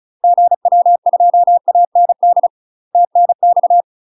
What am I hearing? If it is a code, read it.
Morse code, heard as a keyed tone, about 34 wpm, 700 Hz. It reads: GW2AND TNX
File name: UK MASTER QSOs TRAINING / 35WPM / UK MASTER QSOs 0001